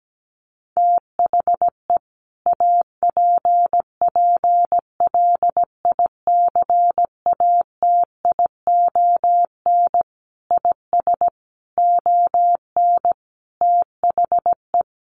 Morse code, heard as a keyed tone, about 17 wpm, 700 Hz. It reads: THE APPLICATION IS ON THE
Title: Sentences from Top 1200 Words 17wpm